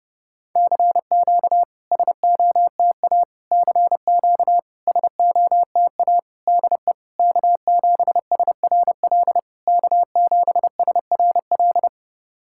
Morse code, heard as a keyed tone, about 30 wpm, 700 Hz. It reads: CQ SOTA CQ SOTA DE K7SRL K7SRL
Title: SOTA QSO Practice 30wpm